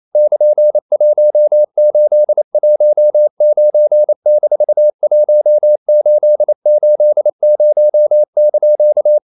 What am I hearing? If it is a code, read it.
Morse code, heard as a keyed tone, about 28 wpm, 600 Hz. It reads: <KN>1819-1880)
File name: CW / Practice / EnglishQuotations0228